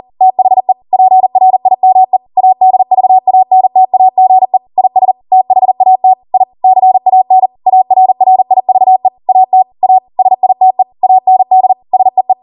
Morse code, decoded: THE PRIME ADVANTAGE IS THAT I CAN ARRIVE AT A SITE AND SEE

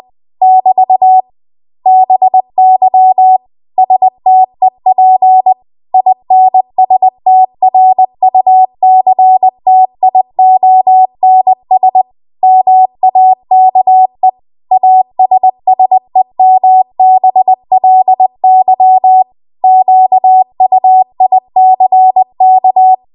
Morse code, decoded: <BT> BY STEP INSTRUCTIONS MAKE ASSEMBLY QUICK

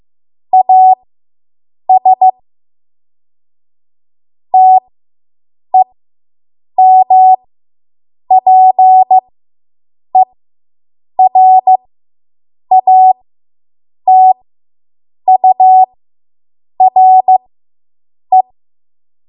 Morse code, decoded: AS TEMPERATURE

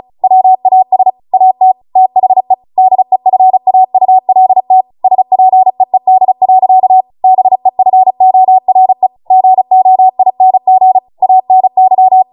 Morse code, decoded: WAS AT THE DEFAULT SPEED. BEFORE GOING ANY

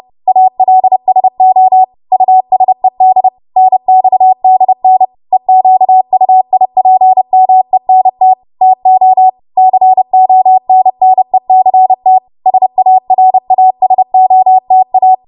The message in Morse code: ALSO USED NXDN EQUIPMENT TO CONNECT SARASOTA

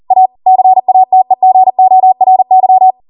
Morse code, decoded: A CATEGORY

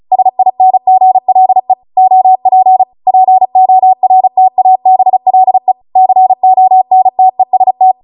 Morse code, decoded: SINGLE OP PORTABLE CONTEST

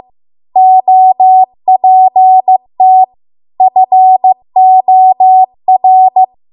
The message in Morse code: OPT FOR

